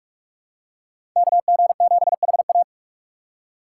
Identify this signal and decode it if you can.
Morse code, heard as a keyed tone, about 46 wpm, 700 Hz. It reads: KG7HA
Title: USA Call Signs - 3x Character Spacing - 45wpm